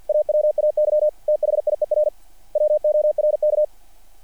Morse code, heard as a keyed tone, about 48 wpm, 600 Hz. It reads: AWAY THIER WORK
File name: archived / CRSnet